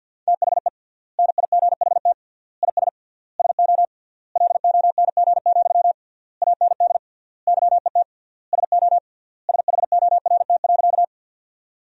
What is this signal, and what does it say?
Morse code, heard as a keyed tone, about 53 wpm, 700 Hz. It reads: THE NIGHT IS SO LONG, AND YET SO SHORT.
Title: 3x Word-Spacing - Sentences from Top 500 Words - 50wpm